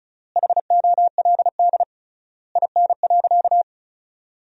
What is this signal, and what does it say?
Morse code, heard as a keyed tone, about 36 wpm, 700 Hz. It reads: HOLD IN.
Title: 3x Word-Spacing - Sentences from Top 400 Words - 35wpm